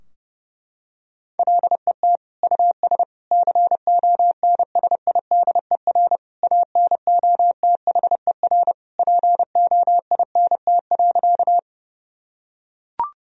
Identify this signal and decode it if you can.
Morse code, heard as a keyed tone, about 32 wpm, 700 Hz. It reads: LET US CONSIDER ANOTHER POINT.
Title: Sentences from Top 1000 Words - Review Part 4 - 30wpm